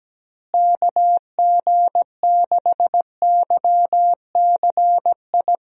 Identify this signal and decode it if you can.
Morse code, heard as a keyed tone, about 17 wpm, 700 Hz. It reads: KG6YCI
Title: USA Call Signs - 6 Characters - Repeated 3x - 17wpm